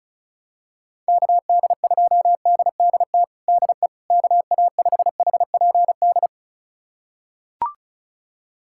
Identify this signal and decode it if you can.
Morse code, heard as a keyed tone, about 34 wpm, 700 Hz. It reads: KD2DDT DE KA5HPD
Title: USA Callsign DE Callsign - 6 Characters - 35wpm